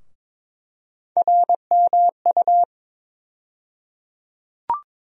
Morse code, heard as a keyed tone, about 22 wpm, 700 Hz. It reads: RMU
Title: USA Call Sign Postfixes - 3 Letters 22wpm